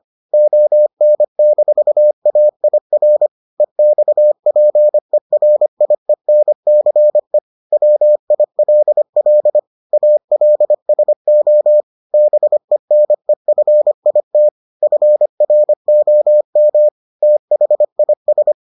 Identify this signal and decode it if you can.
Morse code, heard as a keyed tone, about 25 wpm, 600 Hz. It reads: ON-AIR EXPERIENCE WILL ALSO BENEFIT FROM THIS